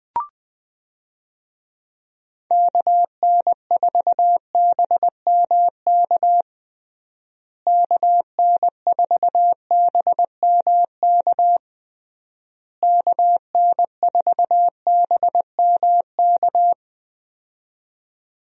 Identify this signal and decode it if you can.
Morse code, heard as a keyed tone, about 20 wpm, 700 Hz. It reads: KN4BMK KN4BMK KN4BMK
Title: USA Call Signs - 6 Characters - Repeated 3x - 20wpm